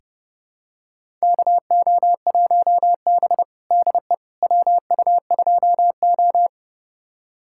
Morse code, heard as a keyed tone, about 30 wpm, 700 Hz. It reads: KO1B DE WU2O
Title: USA Callsign DE Callsign - 4 Characters - 30wpm